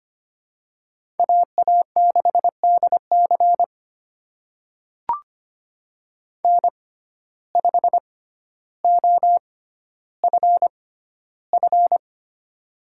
Morse code, heard as a keyed tone, about 25 wpm, spaced as for 13 wpm, 700 Hz. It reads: AA6DC N5OFF
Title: USA Call Signs - 6x Character Spacing - 25wpm